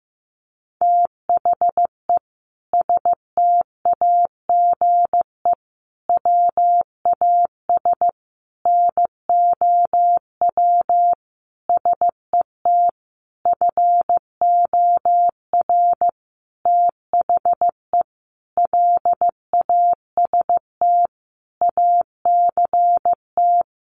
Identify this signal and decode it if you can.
Morse code, heard as a keyed tone, about 15 wpm, 700 Hz. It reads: THE STAGE WAS NOW SET FOR THE LAST ACT
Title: Sentences from Top 1500 Words 15wpm